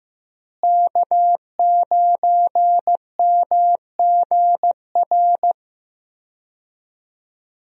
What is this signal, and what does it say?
Morse code, heard as a keyed tone, about 15 wpm, 700 Hz. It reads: K9MGR
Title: USA Call Signs - 5 Characters - Repeated 3x - 15wpm